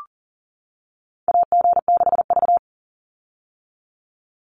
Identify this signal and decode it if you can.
Morse code, heard as a keyed tone, about 41 wpm, 700 Hz. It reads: AG6V
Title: Call Signs - 40wpm